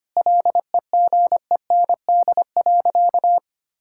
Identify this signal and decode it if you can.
Morse code, heard as a keyed tone, about 25 wpm, 700 Hz. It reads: LEGEND.
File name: practice / 25wpm / Sentences with 2-Word Phrases - 25wpm